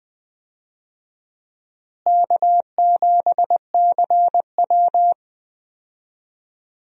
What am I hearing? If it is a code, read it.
Morse code, heard as a keyed tone, about 20 wpm, 700 Hz. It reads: K7CW
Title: USA Call Signs - 4 Characters - Repeated 2x - 20wpm